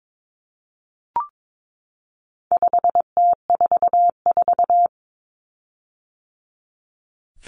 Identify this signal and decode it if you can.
Morse code, heard as a keyed tone, about 22 wpm, 700 Hz. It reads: 5T44
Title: Common Cut Numbers - 4-Digit 22wpm